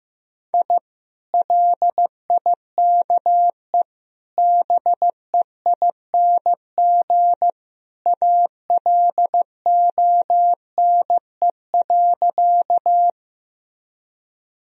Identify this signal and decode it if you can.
Morse code, heard as a keyed tone, about 15 wpm, 700 Hz. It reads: I LIKE BEING ALONE.
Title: Sentences from Top 1000 Words - Review Part 3 - 15wpm